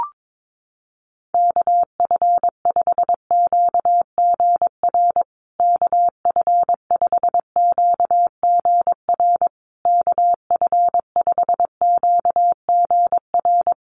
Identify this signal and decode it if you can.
Morse code, heard as a keyed tone, about 22 wpm, 700 Hz. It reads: KF5QGR KF5QGR KF5QGR
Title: USA Call Signs Repeated 3x 22wpm